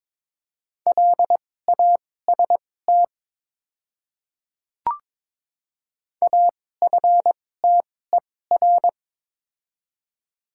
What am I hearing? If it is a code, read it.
Morse code, heard as a keyed tone, about 22 wpm, spaced as for 16 wpm, 700 Hz. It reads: LAST AFTER
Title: 2x Character Spacing - Top 300 Words - 22wpm